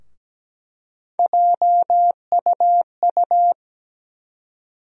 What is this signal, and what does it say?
Morse code, heard as a keyed tone, about 17 wpm, 700 Hz. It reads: JUU
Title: USA Call Sign Postfixes - 3 Letters 17wpm